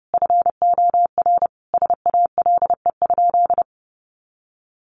Morse code, heard as a keyed tone, about 30 wpm, 700 Hz. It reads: FOR SALE?
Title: Sentences from Top 1400 Words 30wpm